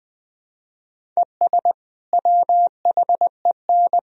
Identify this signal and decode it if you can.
Morse code, heard as a keyed tone, about 20 wpm, 700 Hz. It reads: ES WHEN
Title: Sentences from Top 100 Words and Abbreviations 20wpm